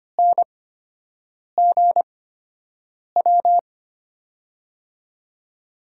Morse code, heard as a keyed tone, about 25 wpm, spaced as for 7 wpm, 700 Hz. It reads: NGW
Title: USA Call Signs - 5 Characters - 8x Character Spacing - 25wpm